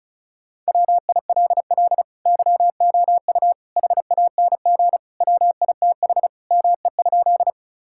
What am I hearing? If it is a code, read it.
Morse code, heard as a keyed tone, about 34 wpm, 700 Hz. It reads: WILL YOU HANG WITH ME?
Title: Sentences from Top 1600 Words 35wpm